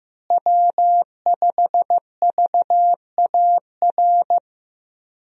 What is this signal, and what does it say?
Morse code, heard as a keyed tone, about 15 wpm, 700 Hz. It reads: W5VAR
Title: USA Call Signs - 5 Characters - Repeated 3x - 15wpm